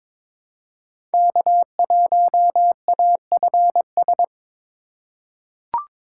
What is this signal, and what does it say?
Morse code, heard as a keyed tone, about 22 wpm, 700 Hz. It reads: K1AFS
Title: USA Call Signs - 5 Characters - 4x Character Spacing - 22wpm